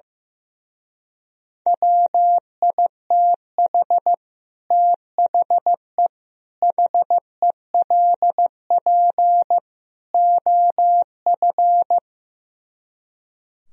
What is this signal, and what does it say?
Morse code, heard as a keyed tone, about 15 wpm, 700 Hz. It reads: WITH THE HELP OF
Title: Speed-Racing - Sets of 4 Words 15wpm